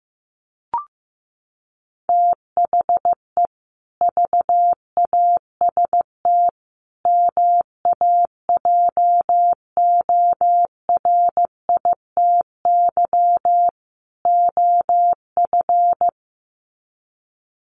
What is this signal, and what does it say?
Morse code, heard as a keyed tone, about 15 wpm, 700 Hz. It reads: THE VAST MAJORITY OF
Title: Sets of 4 Words 15wpm